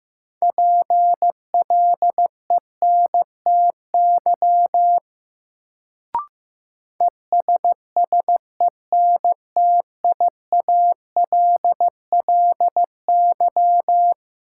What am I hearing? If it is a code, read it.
Morse code, heard as a keyed tone, about 15 wpm, 700 Hz. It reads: PLENTY ESSENTIALLY
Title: Sentences from Top 1300 Words - New Word Drill Alt - 15wpm